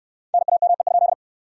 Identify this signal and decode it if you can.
Morse code, heard as a keyed tone, about 69 wpm, 700 Hz. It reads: NAME?